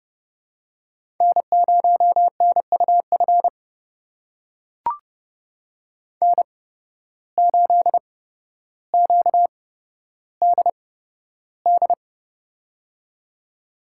Morse code, heard as a keyed tone, about 31 wpm, spaced as for 13 wpm, 700 Hz. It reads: N0NUF N8QDD
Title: USA Call Signs - 5 Characters - 8x Character Spacing - 30wpm